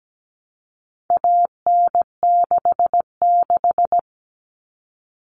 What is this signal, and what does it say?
Morse code, heard as a keyed tone, about 17 wpm, 700 Hz. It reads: AN66
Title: All Cut Numbers - 4-Digit 17wpm